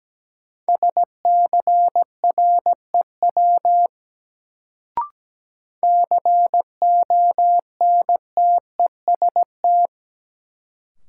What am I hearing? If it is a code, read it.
Morse code, heard as a keyed tone, about 17 wpm, 700 Hz. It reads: SCREW CONTEST
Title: Sentences from Top 1600 Words - New Word Drill Alt - 17wpm